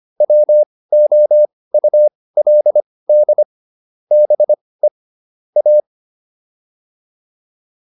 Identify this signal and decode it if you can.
Morse code, heard as a keyed tone, about 25 wpm, spaced as for 18 wpm, 600 Hz. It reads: WOULD BE A
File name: WD204 18